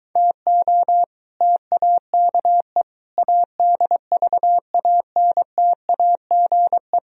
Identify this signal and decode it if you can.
Morse code, heard as a keyed tone, about 23 wpm, 700 Hz. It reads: TO TAKE ADVANTAGE